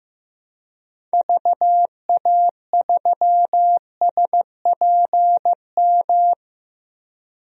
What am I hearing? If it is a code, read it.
Morse code, heard as a keyed tone, about 15 wpm, 700 Hz. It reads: VA3SPM
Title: DXCC-100 Call Signs - 6 Characters - Repeated 3x - 15wpm